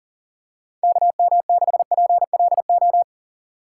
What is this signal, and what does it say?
Morse code, heard as a keyed tone, about 38 wpm, 700 Hz. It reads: KM6PLO
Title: USA Call Signs - 6 Characters - 6x Character Spacing - 40wpm